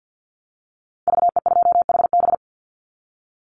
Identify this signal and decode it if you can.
Morse code, heard as a keyed tone, about 50 wpm, 700 Hz. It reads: VE2HB
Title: Call Signs - 50wpm